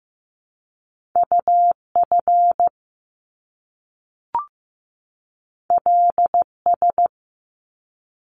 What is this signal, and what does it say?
Morse code, heard as a keyed tone, about 15 wpm, 700 Hz. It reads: UF LS
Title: Top Two Letter Combinations in Words 15wpm